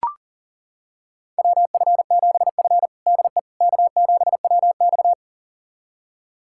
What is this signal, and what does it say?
Morse code, heard as a keyed tone, about 40 wpm, 700 Hz. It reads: WF7F DE K7WX
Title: USA Callsign DE Callsign - 4 Characters - 40wpm